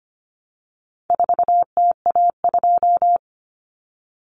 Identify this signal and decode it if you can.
Morse code, heard as a keyed tone, about 25 wpm, 700 Hz. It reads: 4TA2
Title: Common Cut Numbers - 4-Digit 25wpm